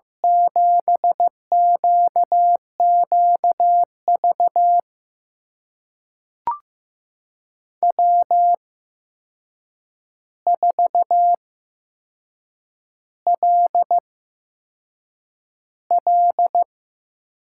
Morse code, heard as a keyed tone, about 16 wpm, spaced as for 6 wpm, 700 Hz. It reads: 7QQV W4LL
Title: USA Call Signs - 5 Characters - 8x Character Spacing - 15wpm